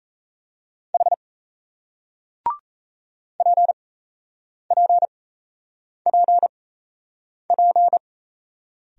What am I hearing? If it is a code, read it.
Morse code, heard as a keyed tone, about 37 wpm, spaced as for 8 wpm, 700 Hz. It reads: H PPPP